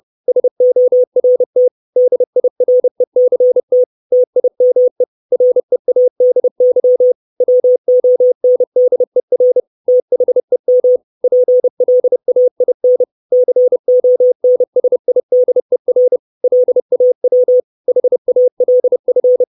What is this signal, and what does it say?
Morse code, heard as a keyed tone, about 30 wpm, 500 Hz. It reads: SORT DIRECT TIME READY WONDER THEM PLAIN CONSIDER LAW HALF
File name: 632 Words 30WPM